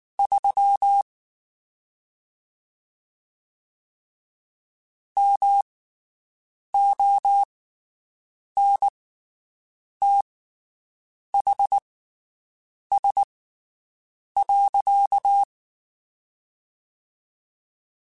Morse code, heard as a keyed tone, about 19 wpm, spaced as for 7 wpm, 800 Hz. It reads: 3 MONTHS.
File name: (spaced) / QSOs Spaced-mp3s / WD0BGP 1 @18 sp